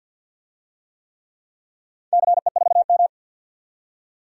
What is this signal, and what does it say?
Morse code, heard as a keyed tone, about 50 wpm, 700 Hz. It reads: CE4M
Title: DXCC-100 Call Signs - 4 Characters - 50wpm